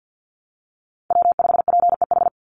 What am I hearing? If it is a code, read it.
Morse code, heard as a keyed tone, about 50 wpm, 700 Hz. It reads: W5PEH
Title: Call Signs Repeated 3x - 50wpm